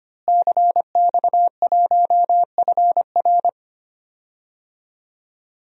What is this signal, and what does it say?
Morse code, heard as a keyed tone, about 25 wpm, 700 Hz. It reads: CX1FR
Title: DXCC-100 Call Signs - 5 Characters - Repeated 2x - 25wpm